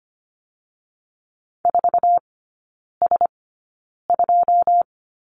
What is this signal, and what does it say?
Morse code, heard as a keyed tone, about 25 wpm, spaced as for 9 wpm, 700 Hz. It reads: 4S2
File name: Lesson 14e - Number 2 - Three Characters 25wpm